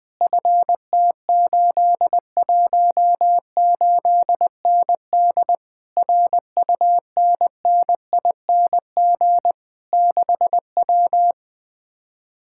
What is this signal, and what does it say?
Morse code, heard as a keyed tone, about 20 wpm, 700 Hz. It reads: FT818ND RUNNING 6W
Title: Speed-Racing Common QSO Phrases - 17wpm